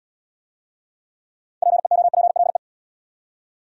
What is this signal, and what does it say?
Morse code, heard as a keyed tone, about 78 wpm, 700 Hz. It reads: PEOPLE